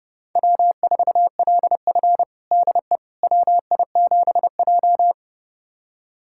Morse code, heard as a keyed tone, about 30 wpm, 700 Hz. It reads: W4LF DE WI7J
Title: USA Callsign DE Callsign - 4 Characters - 30wpm